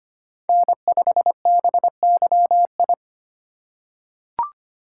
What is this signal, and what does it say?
Morse code, heard as a keyed tone, about 25 wpm, 700 Hz. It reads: N5BYI
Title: USA Call Signs - 4x Character Spacing - 25wpm